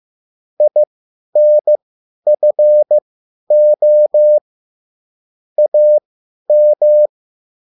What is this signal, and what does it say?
Morse code, heard as a keyed tone, about 15 wpm, spaced as for 10 wpm, 600 Hz. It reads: INFO AM
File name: cwbook